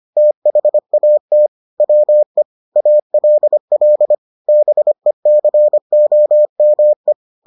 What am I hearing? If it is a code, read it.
Morse code, heard as a keyed tone, about 25 wpm, 600 Hz. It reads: THAT WE ALL BECOME